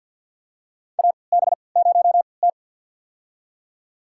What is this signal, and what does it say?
Morse code, heard as a keyed tone, about 50 wpm, spaced as for 28 wpm, 700 Hz. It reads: AB0T
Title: USA Call Signs - 4 Characters - 3x Character Spacing - 50wpm